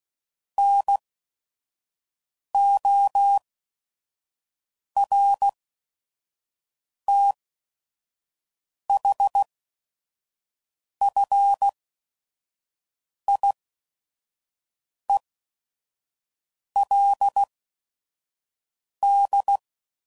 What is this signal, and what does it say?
Morse code, heard as a keyed tone, about 16 wpm, spaced as for 5 wpm, 800 Hz. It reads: NORTHFIELD
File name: CW Code Groups @5WPM / fromSM's / Cities